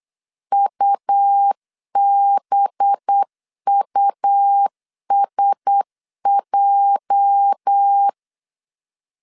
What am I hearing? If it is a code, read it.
Morse code, heard as a keyed tone, about 8 wpm, 800 Hz. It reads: UBUSJ
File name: CW / DET / 1 9 1500C S B 8wpm 5624